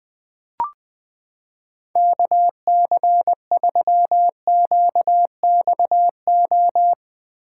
Morse code, heard as a keyed tone, about 20 wpm, 700 Hz. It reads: KC3QXO
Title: 2023 POTA Activator Call Signs - 20wpm